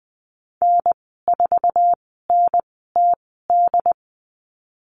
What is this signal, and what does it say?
Morse code, heard as a keyed tone, about 20 wpm, spaced as for 15 wpm, 700 Hz. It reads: N4NTD
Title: 069 Lesson 12g - Letter d - Call Signs 20wpm